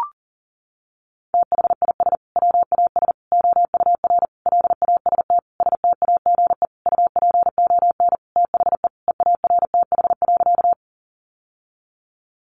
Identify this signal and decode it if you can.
Morse code, heard as a keyed tone, about 40 wpm, 700 Hz. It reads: THIS WAS OUR LAST STAGE UPON THE EARTH.
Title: Sentences from Top 1200 Words 40wpm